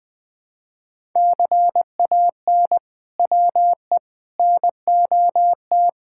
Morse code, heard as a keyed tone, about 21 wpm, 700 Hz. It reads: CAN WE NOT
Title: Sentences from Top 1000 Words - Review Part 6 - 20wpm